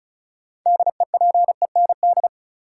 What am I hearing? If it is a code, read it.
Morse code, heard as a keyed tone, about 34 wpm, 700 Hz. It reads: DEPEND
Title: Sentences from Top 1300 Words - New Word Drill - 35wpm